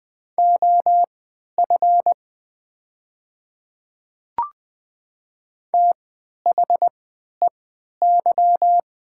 Morse code, heard as a keyed tone, about 20 wpm, spaced as for 11 wpm, 700 Hz. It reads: OF THEY
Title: 3x Character Spacing - Top 200 Words - 20wpm